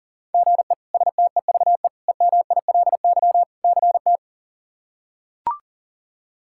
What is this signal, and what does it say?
Morse code, heard as a keyed tone, about 37 wpm, 700 Hz. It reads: GE STEVE EMILY CT
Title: K1USN SST QSO Practice - 40wpm